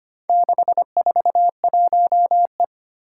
Morse code, heard as a keyed tone, about 25 wpm, 690 Hz. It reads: 641E
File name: practice / 25wpm / Four Letter-Number Combinations - 25wpm